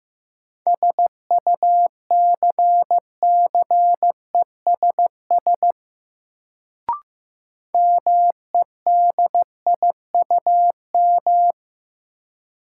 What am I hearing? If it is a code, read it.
Morse code, heard as a keyed tone, about 15 wpm, 700 Hz. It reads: SUCCESS MEDIUM
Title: Sentences from Top 900 Words - New Word Drill - 15wpm